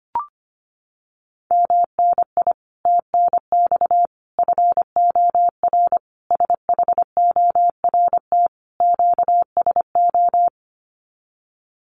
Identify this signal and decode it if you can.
Morse code, heard as a keyed tone, about 25 wpm, 700 Hz. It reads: MNI TNX FOR SHORT QSO
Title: Common QSO Phrases 25wpm